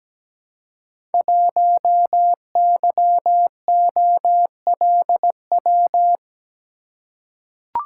Morse code, heard as a keyed tone, about 17 wpm, 700 Hz. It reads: 1YOLW
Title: Code Groups - 5 Letter:Numbers - 17wpm